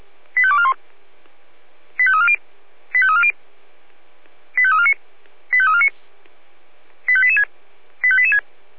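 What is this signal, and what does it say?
Five-tone selective call (ZVEI), heard as quick runs of five tones, tones of about 70 ms. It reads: 85241 85340 85340 85349 85349 85886 85886
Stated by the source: ZVEI five-tone selective call received on an SDR.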